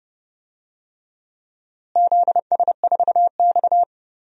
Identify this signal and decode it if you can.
Morse code, heard as a keyed tone, about 30 wpm, 700 Hz. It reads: ZS4X
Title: DXCC-100 Call Signs - 4 Characters - Repeated 2x - 30wpm